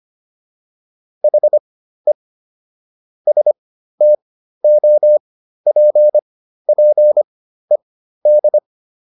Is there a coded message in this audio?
Yes: Morse code from 1.2 to 8.6 s, heard as a keyed tone, about 25 wpm, spaced as for 13 wpm, 600 Hz. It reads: HE STOPPED